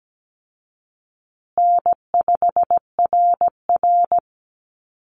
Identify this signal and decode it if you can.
Morse code, heard as a keyed tone, about 17 wpm, 700 Hz. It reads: N5RR
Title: Call Signs Repeated 2x - 17wpm